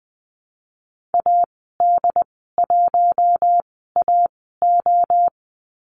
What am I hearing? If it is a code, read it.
Morse code, heard as a keyed tone, about 20 wpm, spaced as for 14 wpm, 700 Hz. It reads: AD1AO
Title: 088 Lesson 15g - Number 5 - Call Signs 20wpm